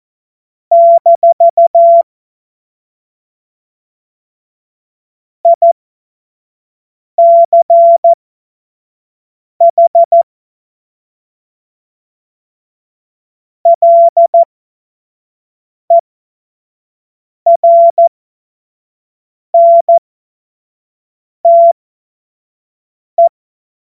Morse code, heard as a keyed tone, about 14 wpm, spaced as for 5 wpm, 680 Hz. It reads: - ICH LERNTE
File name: Morsekurs / Witze / Witz 18. Januar 2022, 25 F70